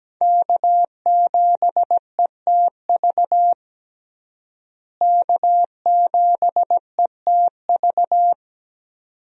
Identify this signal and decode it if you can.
Morse code, heard as a keyed tone, about 17 wpm, 700 Hz. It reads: K7ETV K7ETV
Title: USA Call Signs - 5 Characters - Repeated 3x - 17wpm